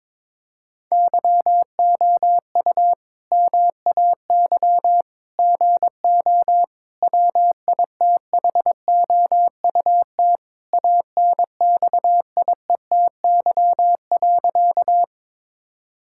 Morse code, heard as a keyed tone, about 21 wpm, 700 Hz. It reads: YOU MAY GO WITHOUT ANXIETY.
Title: Sentences from Top 1600 Words 22wpm